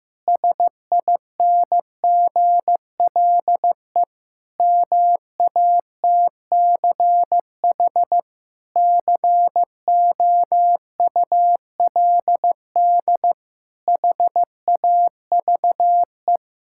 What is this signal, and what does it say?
Morse code, heard as a keyed tone, about 14 wpm, 700 Hz. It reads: SINGLE MATCH COULD HAVE
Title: Sentences from Top 1600 Words 15wpm